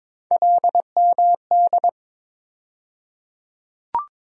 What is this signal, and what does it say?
Morse code, heard as a keyed tone, about 22 wpm, 700 Hz. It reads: LMD
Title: USA Call Sign Postfixes - 3 Letters 22wpm